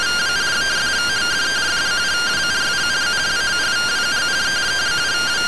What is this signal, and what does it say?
Two-tone teleprinter signal, about 170 Hz shift.